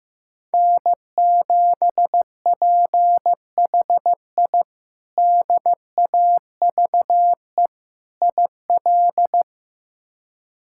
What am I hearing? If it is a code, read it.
Morse code, heard as a keyed tone, about 15 wpm, 700 Hz. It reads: N7PHI DAVE IL
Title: K1USN SST QSO Practice - 15wpm